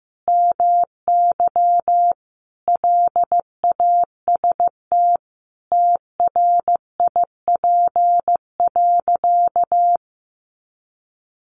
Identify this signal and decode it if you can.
Morse code, heard as a keyed tone, about 15 wpm, 700 Hz. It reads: MY LAST TRIP.
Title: Sentences from Top 1300 Words 15wpm